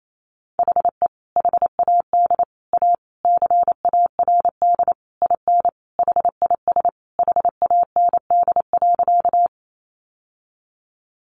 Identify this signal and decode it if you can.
Morse code, heard as a keyed tone, about 28 wpm, 700 Hz. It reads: HE HAD A CARD IN HIS HAND.
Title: Sentences from Top 600 Words 28wpm